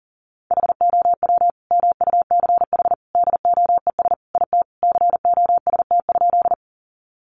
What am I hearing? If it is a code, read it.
Morse code, heard as a keyed tone, about 40 wpm, 700 Hz. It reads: HOW MUCH DOES IT COST?
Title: Sentences from Top 800 Words 40wpm